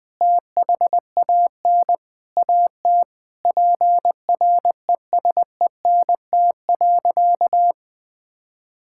Morse code, heard as a keyed tone, about 21 wpm, 700 Hz. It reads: THAN AT PRESENT.
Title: Sentences from Top 1000 Words - Review Part 5 - 20wpm